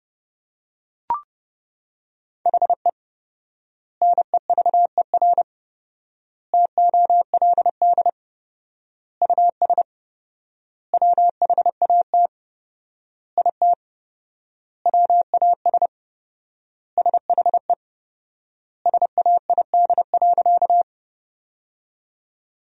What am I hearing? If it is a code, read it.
Morse code, heard as a keyed tone, about 29 wpm, 700 Hz. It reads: HE NEVER TOLD US WHAT IT WAS SHE SAID.
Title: 4x Word-Spacing - Sentences from Top 400 Words - 30wpm